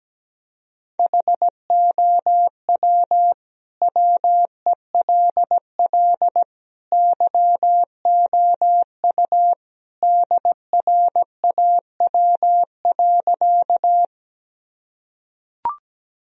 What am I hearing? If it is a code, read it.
Morse code, heard as a keyed tone, about 17 wpm, 700 Hz. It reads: HOW WELL YOU DRAW.
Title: Sentences from Top 1000 Words - Review Part 1 - 17wpm